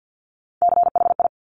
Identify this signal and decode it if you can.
Morse code, heard as a keyed tone, about 50 wpm, 700 Hz. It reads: CHI
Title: Top Three Letter Combinations in Words 50wpm